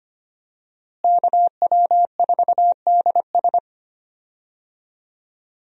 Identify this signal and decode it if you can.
Morse code, heard as a keyed tone, about 25 wpm, 700 Hz. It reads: KW4DS
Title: USA Call Signs - 5 Characters - Repeated 2x - 25wpm